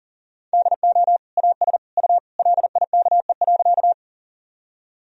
Morse code, heard as a keyed tone, about 40 wpm, 700 Hz. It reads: DO AS U LIKE.
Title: Sentences from Top 100 Words and Abbreviations 40wpm